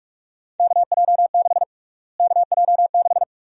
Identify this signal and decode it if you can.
Morse code, heard as a keyed tone, about 46 wpm, 700 Hz. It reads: KJ6 KJ6
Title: USA Call Sign Prefixes - Repeated 2x - 45wpm